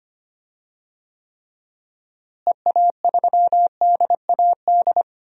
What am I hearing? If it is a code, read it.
Morse code, heard as a keyed tone, about 25 wpm, 700 Hz. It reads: EA3DAD
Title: DXCC-100 Call Signs - 6 Characters - Repeated 3x - 25wpm